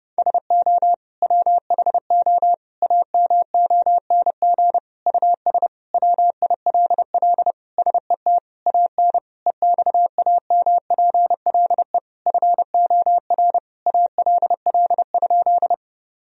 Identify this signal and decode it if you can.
Morse code, heard as a keyed tone, about 32 wpm, 700 Hz. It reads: SO WHO AMONG US WILL SET AN EXAMPLE FOR ALL?
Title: Sentences from Top 1000 Words - Review Part 3 - 30wpm